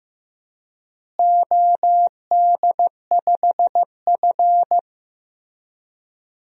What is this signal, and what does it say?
Morse code, heard as a keyed tone, about 15 wpm, 700 Hz. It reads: OD5F
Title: DXCC-100 Call Signs - 4 Characters - Repeated 3x - 15wpm